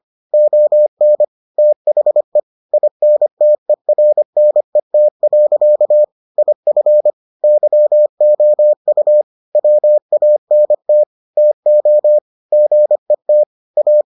Morse code, heard as a keyed tone, about 25 wpm, 600 Hz. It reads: ON THE INTERNET. IF YOU WANT TO GET A